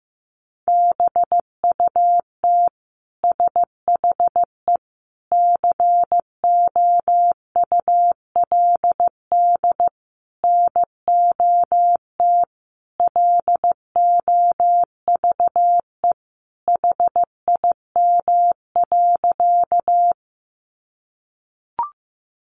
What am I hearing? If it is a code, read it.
Morse code, heard as a keyed tone, about 14 wpm, 700 Hz. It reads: BUT SHE COULD NOT LOVE HIM.
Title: Sentences from Top 600 Words 15wpm